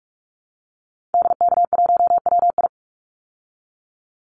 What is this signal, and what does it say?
Morse code, heard as a keyed tone, about 44 wpm, 700 Hz. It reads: DK1WI
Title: Call Signs Repeated 2x - 45wpm